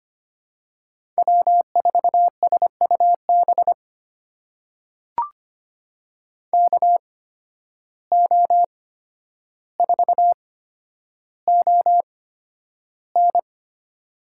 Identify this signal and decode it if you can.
Morse code, heard as a keyed tone, about 25 wpm, spaced as for 11 wpm, 700 Hz. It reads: W4SUB KO4ON
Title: USA Call Signs - 5 Characters - 8x Character Spacing - 25wpm